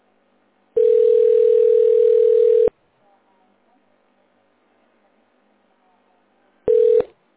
Ringback tone (North America), 440 + 480 Hz.